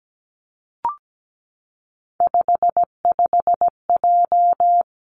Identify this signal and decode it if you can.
Morse code, heard as a keyed tone, about 17 wpm, 700 Hz. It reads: 55J
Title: Three Challenging Letter-Number Combinations 17wpm